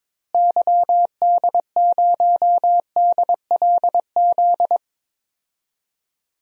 Morse code, heard as a keyed tone, about 22 wpm, 700 Hz. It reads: YD0DLZ
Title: DXCC-100 Call Signs - 6 Characters - Repeated 3x - 22wpm